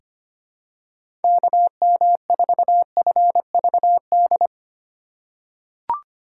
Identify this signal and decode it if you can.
Morse code, heard as a keyed tone, about 25 wpm, 700 Hz. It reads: KM4FVD
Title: USA Call Signs - 6 Characters - 4x Character Spacing - 25wpm